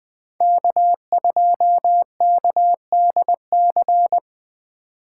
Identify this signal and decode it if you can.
Morse code, heard as a keyed tone, about 20 wpm, 700 Hz. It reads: K2KDC